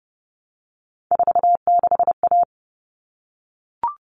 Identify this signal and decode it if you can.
Morse code, heard as a keyed tone, about 30 wpm, 700 Hz. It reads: 46A
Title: All Cut Numbers - 3-Digit 30wpm